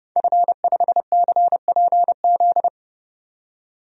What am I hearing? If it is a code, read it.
Morse code, heard as a keyed tone, about 30 wpm, 700 Hz. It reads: F5CPZ
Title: DXCC-100 Call Signs - 5 Characters - 30wpm